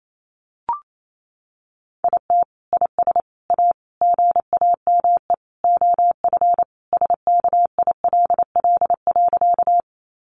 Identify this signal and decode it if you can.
Morse code, heard as a keyed tone, about 28 wpm, 700 Hz. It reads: IT IS A GAME OF SKILL.
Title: Sentences from Top 1200 Words 28wpm